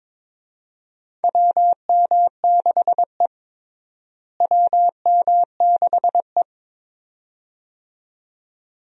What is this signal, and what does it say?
Morse code, heard as a keyed tone, about 22 wpm, 700 Hz. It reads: WM6E WM6E
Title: USA Call Signs - 4 Characters - Repeated 2x - 22wpm